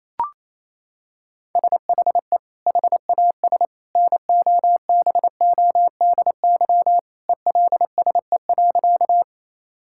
Morse code, heard as a keyed tone, about 28 wpm, 700 Hz. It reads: SHE HAS NOBODY ELSE.
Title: Sentences from Top 1600 Words 28wpm